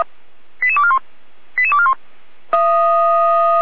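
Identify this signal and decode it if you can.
Five-tone selective call (ZVEI), heard as quick runs of five tones, tones of about 70 ms. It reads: 88241 88241
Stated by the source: ZVEI five-tone call received by SDR.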